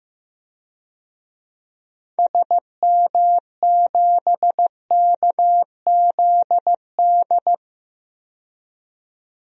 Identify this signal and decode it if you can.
Morse code, heard as a keyed tone, about 15 wpm, 700 Hz. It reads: SM7KZD
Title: DXCC-100 Call Signs - 6 Characters - 15wpm